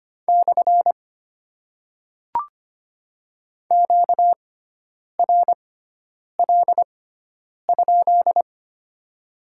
Morse code, heard as a keyed tone, about 25 wpm, spaced as for 9 wpm, 700 Hz. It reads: / QRL?
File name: Top 90 QSO Elements - 6x Character Spacing - 25wpm